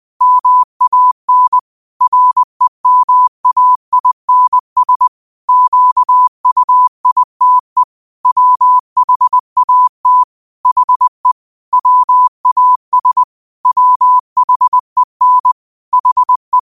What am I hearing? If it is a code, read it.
Morse code, heard as a keyed tone, about 20 wpm, 1000 Hz. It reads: MAN REMAINS QUITE WHAT HE WAS WHEN HE